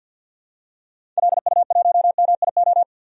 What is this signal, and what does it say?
Morse code, heard as a keyed tone, about 46 wpm, 700 Hz. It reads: LU1MIO
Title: DXCC-100 Call Signs - 6 Characters - 50wpm